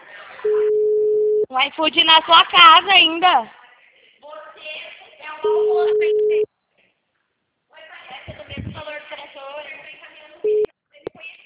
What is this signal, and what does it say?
Ringback tone (Europe, 425 Hz), with speech.
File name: out- -2023- - - .23682